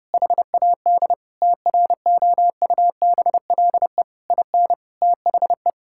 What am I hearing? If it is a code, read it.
Morse code, heard as a keyed tone, about 30 wpm, 700 Hz. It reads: HAD TROUBLE IN THE
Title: Sentences from Top 1600 Words 30wpm